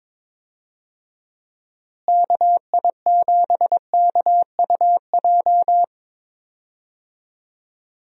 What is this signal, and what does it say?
Morse code, heard as a keyed tone, about 22 wpm, 700 Hz. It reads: KI7KUJ
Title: USA Call Signs - 6 Characters - Repeated 3x - 22wpm